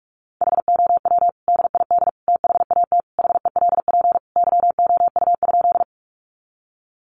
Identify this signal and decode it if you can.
Morse code, heard as a keyed tone, about 44 wpm, 700 Hz. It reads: HOW DID THAT HELP YOU?
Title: Sentences from Top 200 Words 45wpm